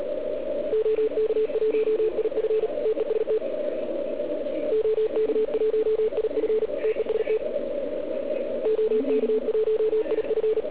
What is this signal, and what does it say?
Morse code, heard as a keyed tone, about 36 wpm, 440 Hz. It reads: OK1IF TEST OK1IF TEST OK1IF